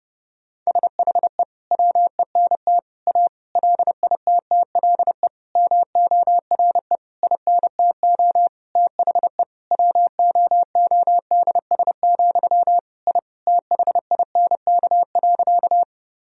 Morse code, heard as a keyed tone, about 28 wpm, 700 Hz. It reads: SHE WENT A LITTLE MORE INTO THE WOODS, I THINK.
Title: Sentences from Top 1000 Words - Review Part 2 - 30wpm